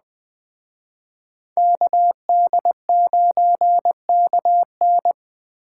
Morse code, heard as a keyed tone, about 20 wpm, 700 Hz. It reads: KD9KN
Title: Speed Racing - USA 2x2 Call Signs 17wpm